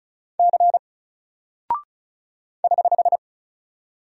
Morse code, heard as a keyed tone, about 36 wpm, 700 Hz. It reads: C <HH>
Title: Single Character (Letter:Number:Punctuation:Prosign) Alt 35wpm